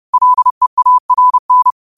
Morse code, heard as a keyed tone, about 30 wpm, 1000 Hz. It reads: LEARN
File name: qotdfiles / 30wpm